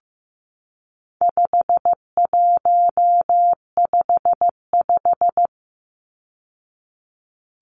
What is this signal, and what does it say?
Morse code, heard as a keyed tone, about 15 wpm, 700 Hz. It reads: 5155
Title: Numbers - 4-Digit 15wpm